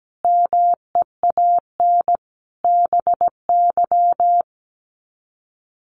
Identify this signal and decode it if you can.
Morse code, heard as a keyed tone, about 17 wpm, 700 Hz. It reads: MEAN BY
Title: Sets of 5 Words 17wpm